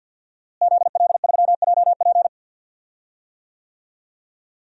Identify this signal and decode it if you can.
Morse code, heard as a keyed tone, about 52 wpm, 700 Hz. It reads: ZL3JP
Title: DXCC-100 Call Signs - 5 Characters - Repeated 2x - 50wpm